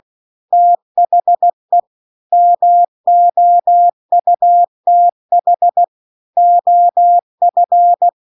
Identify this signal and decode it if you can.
Morse code, heard as a keyed tone, about 16 wpm, 700 Hz. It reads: THE MOUTH OF
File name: 16WPM